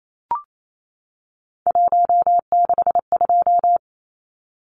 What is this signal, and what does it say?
Morse code, heard as a keyed tone, about 28 wpm, 700 Hz. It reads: 162
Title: Numbers - 3-Digit 28wpm